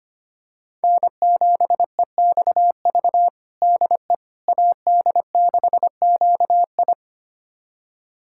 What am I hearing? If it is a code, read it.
Morse code, heard as a keyed tone, about 25 wpm, 700 Hz. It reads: N7EXV DE AD6QI
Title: USA Callsign DE Callsign - 5 Characters - 25wpm